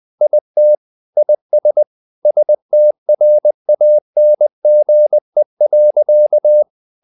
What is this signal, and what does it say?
Morse code, heard as a keyed tone, about 20 wpm, 600 Hz. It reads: IT IS STRANGE.
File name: the old man and the sea part 2 20wpm